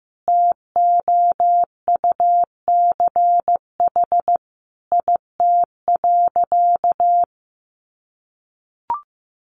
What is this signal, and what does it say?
Morse code, heard as a keyed tone, about 15 wpm, 700 Hz. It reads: TOUCH IT.
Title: Sentences from Top 1200 Words 15wpm